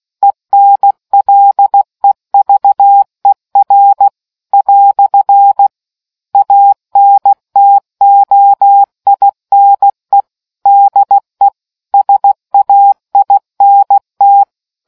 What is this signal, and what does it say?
Morse code, heard as a keyed tone, about 16 wpm, 800 Hz. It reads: ENLEVER " ANTOINE DE SAINT